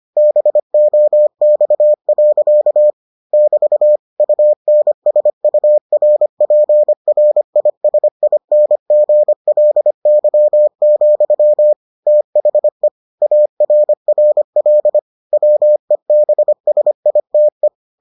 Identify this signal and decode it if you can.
Morse code, heard as a keyed tone, about 25 wpm, 600 Hz. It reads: BOX. <BT> UNSURPRISINGLY, THE ARRL WEBSITE